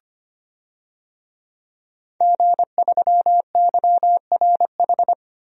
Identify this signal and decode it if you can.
Morse code, heard as a keyed tone, about 25 wpm, 700 Hz. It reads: G3YRH
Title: DXCC-100 Call Signs - 5 Characters - 25wpm